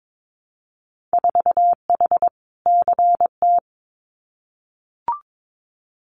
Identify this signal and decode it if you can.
Morse code, heard as a keyed tone, about 22 wpm, 700 Hz. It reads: 4H CT
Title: Winter Field Day Exchanges Encore 22wpm